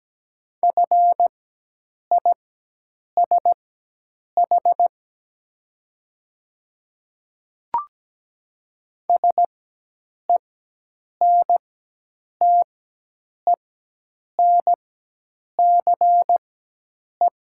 Morse code, heard as a keyed tone, about 17 wpm, spaced as for 8 wpm, 700 Hz. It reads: FISH SENTENCE
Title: 4x Character Spacing - Top 300 Words - 17wpm